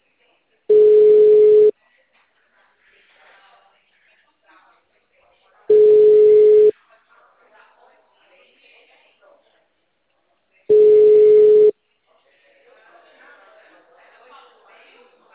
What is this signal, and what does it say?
Ringback tone (Europe, 425 Hz), with speech.
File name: out- -2004- - - .34